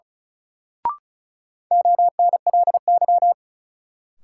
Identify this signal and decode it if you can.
Morse code, heard as a keyed tone, about 36 wpm, 700 Hz. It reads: ONLY
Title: Sentences from Top 200 Words - New Word Drill - 35wpm